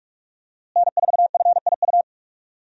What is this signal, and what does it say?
Morse code, heard as a keyed tone, about 46 wpm, 700 Hz. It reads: N4FIU
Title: USA Call Signs - 5 Characters - Repeated 2x - 45wpm